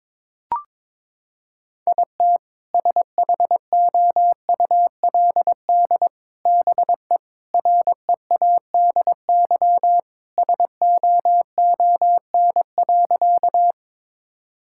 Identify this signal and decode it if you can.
Morse code, heard as a keyed tone, about 22 wpm, 700 Hz. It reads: IT SHOULD BE READY SOON.
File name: practice / 22wpm / Sentences with 2-Word Phrases - 22wpm